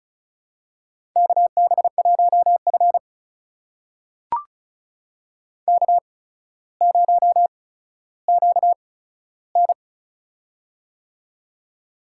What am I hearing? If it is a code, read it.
Morse code, heard as a keyed tone, about 36 wpm, spaced as for 15 wpm, 700 Hz. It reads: KB1F K0QN
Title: USA Call Signs - 4 Characters - 8x Character Spacing - 35wpm